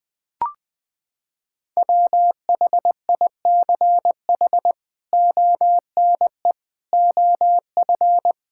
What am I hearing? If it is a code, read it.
Morse code, heard as a keyed tone, about 20 wpm, 700 Hz. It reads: WHICH ONE OF
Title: Sentences from Top 1000 Words - Review Part 5 - 20wpm